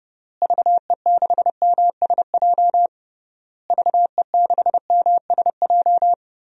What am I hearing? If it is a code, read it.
Morse code, heard as a keyed tone, about 30 wpm, 700 Hz. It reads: VE6MSJ VE6MSJ
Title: DXCC-100 Call Signs - 6 Characters - Repeated 3x - 30wpm